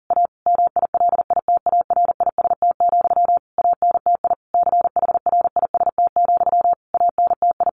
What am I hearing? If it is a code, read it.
Morse code, heard as a keyed tone, about 40 wpm, 700 Hz. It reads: A MILITARIST, ANTI CHRIST, ANTI
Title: A Sound Of Thunder 40wpm